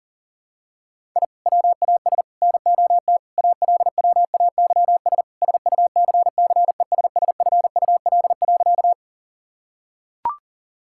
Morse code, heard as a keyed tone, about 37 wpm, 700 Hz. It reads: I WAS NOT ALWAYS SUCCESSFUL.
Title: Sentences from Top 1000 Words - Review Part 6 - 40wpm